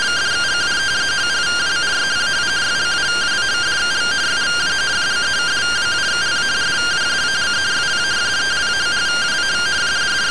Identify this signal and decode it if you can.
Two-tone teleprinter signal, about 170 Hz shift.